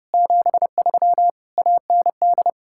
Morse code, heard as a keyed tone, about 37 wpm, 700 Hz. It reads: 73 AND
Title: Speed-Racing Common QSO Phrases - 30wpm